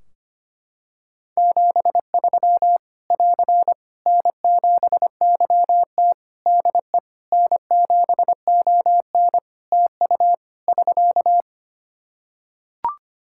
Morse code, heard as a keyed tone, about 25 wpm, 700 Hz. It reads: 73 <AR> N7YT DE N7ON TU <SK>
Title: Practice QSO - Exchange 4 - Ending - 25wpm